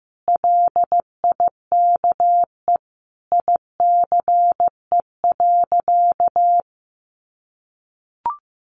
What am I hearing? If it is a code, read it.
Morse code, heard as a keyed tone, about 15 wpm, 700 Hz. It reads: LIKE ICE.
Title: Sentences from Top 1000 Words 15wpm